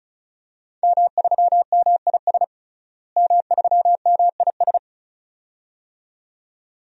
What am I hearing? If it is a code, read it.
Morse code, heard as a keyed tone, about 36 wpm, 700 Hz. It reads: M3MIS M3MIS
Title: DXCC-100 Call Signs - 5 Characters - Repeated 2x - 35wpm